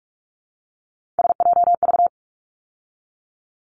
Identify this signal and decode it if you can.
Morse code, heard as a keyed tone, about 44 wpm, 700 Hz. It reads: SJV
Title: ARRL Sections 45wpm